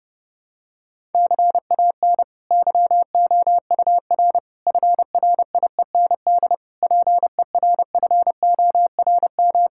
Morse code, heard as a keyed tone, about 30 wpm, 700 Hz. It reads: CAN YOUR FRIEND PERFORM
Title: Sentences from Top 1000 Words - Review Part 5 - 30wpm